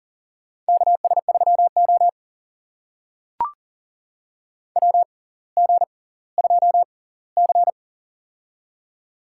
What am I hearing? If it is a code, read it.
Morse code, heard as a keyed tone, about 40 wpm, spaced as for 20 wpm, 700 Hz. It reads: KS3O WG2C
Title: USA Call Signs - 4 Characters - 6x Character Spacing - 40wpm